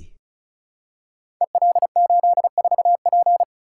Morse code, heard as a keyed tone, about 36 wpm, 690 Hz. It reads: EL84P
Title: Code Groups - 5 Letter:Numbers - 35wpm